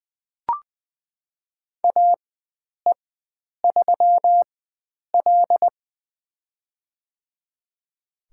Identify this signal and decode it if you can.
Morse code, heard as a keyed tone, about 20 wpm, spaced as for 9 wpm, 700 Hz. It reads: AE3L
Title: USA Call Signs - 4x Character Spacing - 20wpm